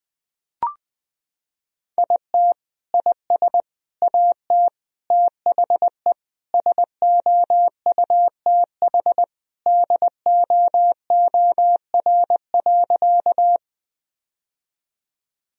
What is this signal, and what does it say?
Morse code, heard as a keyed tone, about 20 wpm, 700 Hz. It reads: IT IS AT THE SOUTH DOOR.
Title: Sentences from Top 1000 Words - Review Part 2 - 20wpm